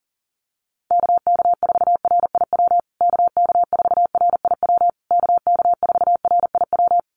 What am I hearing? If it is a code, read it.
Morse code, heard as a keyed tone, about 40 wpm, 700 Hz. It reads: KK4RIW KK4RIW KK4RIW
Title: USA Call Signs Repeated 3x 40wpm